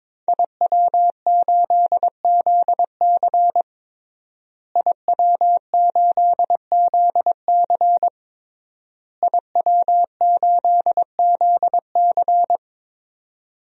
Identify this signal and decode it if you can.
Morse code, heard as a keyed tone, about 22 wpm, 700 Hz. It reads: IW8ZC IW8ZC IW8ZC
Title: DXCC-100 Call Signs - 5 Characters - Repeated 3x - 22wpm